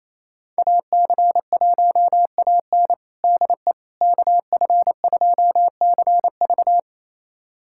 Morse code, heard as a keyed tone, about 27 wpm, 700 Hz. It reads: AC1AN DE KF2CV
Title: USA Callsign DE Callsign - 5 Characters - 28wpm